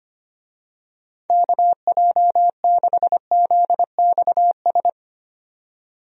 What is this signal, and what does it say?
Morse code, heard as a keyed tone, about 25 wpm, 700 Hz. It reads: KJ6ZXS
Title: USA Call Signs - 6 Characters - 25wpm